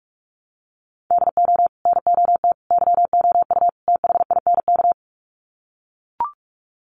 Morse code, heard as a keyed tone, about 44 wpm, 700 Hz. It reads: DO NOT YOU THINK
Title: Sets of 4 Words 45wpm